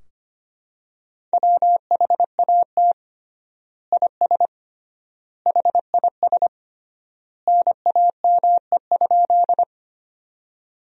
Morse code, heard as a keyed tone, about 25 wpm, 700 Hz. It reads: WHAT IS HIS NAME?
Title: 3x Word-Spacing - Sentences from Top 200 Words - 25wpm